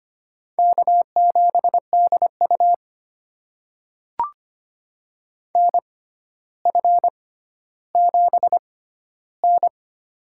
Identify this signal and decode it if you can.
Morse code, heard as a keyed tone, about 25 wpm, spaced as for 13 wpm, 700 Hz. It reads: K7DU NF7N
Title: USA Call Signs - 4 Characters - 6x Character Spacing - 25wpm